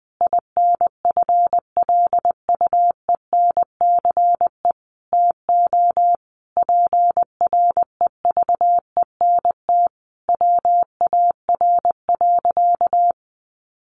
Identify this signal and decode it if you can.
Morse code, heard as a keyed tone, about 20 wpm, 700 Hz. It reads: INFLUENCE TO PREVENT WAR.
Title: Sentences from Top 1100 Words 20wpm